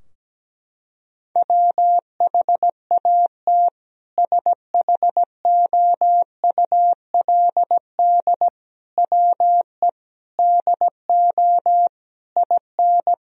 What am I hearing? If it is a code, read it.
Morse code, heard as a keyed tone, about 17 wpm, 700 Hz. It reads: WHAT SHOULD WE DO IN
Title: Sentences from Top 1000 Words - Review Part 4 - 17wpm